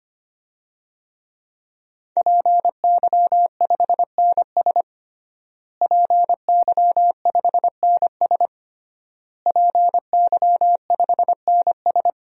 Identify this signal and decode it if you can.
Morse code, heard as a keyed tone, about 25 wpm, 700 Hz. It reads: PY5NS PY5NS PY5NS
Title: DXCC-100 Call Signs - 5 Characters - Repeated 3x - 25wpm